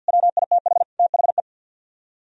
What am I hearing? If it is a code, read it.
Morse code, heard as a keyed tone, about 64 wpm, 700 Hz. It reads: WITH THE